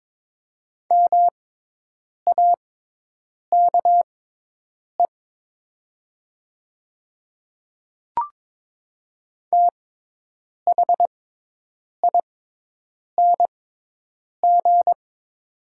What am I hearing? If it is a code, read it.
Morse code, heard as a keyed tone, about 22 wpm, spaced as for 8 wpm, 700 Hz. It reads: MAKE THING
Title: 6x Character Spacing - Top 300 Words - 22wpm